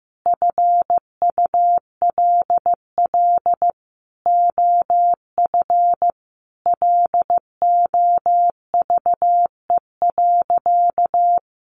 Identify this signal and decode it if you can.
Morse code, heard as a keyed tone, about 15 wpm, 700 Hz. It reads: FULL OF LOVE.
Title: Sentences from Top 1500 Words 15wpm